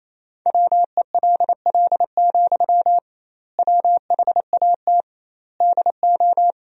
Morse code, heard as a keyed tone, about 28 wpm, 700 Hz. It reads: WELL, WHAT DO
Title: 2x Word-Spacing - Sentences from Top 200 Words - 28wpm